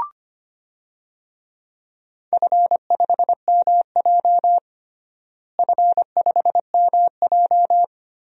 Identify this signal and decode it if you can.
Morse code, heard as a keyed tone, about 25 wpm, 700 Hz. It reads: F5MJ F5MJ
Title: DXCC-100 Call Signs - 4 Characters - Repeated 2x - 25wpm